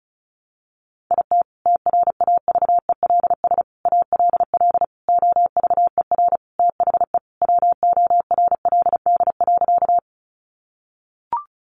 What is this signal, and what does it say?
Morse code, heard as a keyed tone, about 34 wpm, 700 Hz. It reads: IT TRAVELS ALL OVER THE WORLD.
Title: Additional Easy Phrases 35wpm